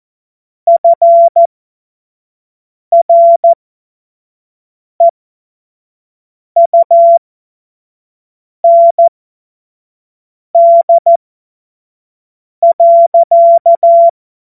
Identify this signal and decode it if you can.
Morse code, heard as a keyed tone, about 14 wpm, spaced as for 5 wpm, 680 Hz. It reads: FREUND.
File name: Morsekurs / Witze / Witz 10. Mai 2022 25 F70